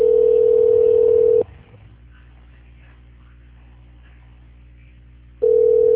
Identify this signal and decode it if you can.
Ringback tone (North America), 440 + 480 Hz.